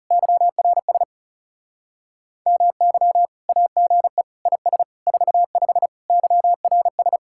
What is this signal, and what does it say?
Morse code, heard as a keyed tone, about 38 wpm, 700 Hz. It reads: YRS MY AGE IS 45 YRS